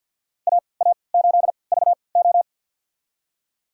Morse code, heard as a keyed tone, about 50 wpm, spaced as for 28 wpm, 700 Hz. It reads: AA8VO
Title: USA Call Signs - 5 Characters - 3x Character Spacing - 50wpm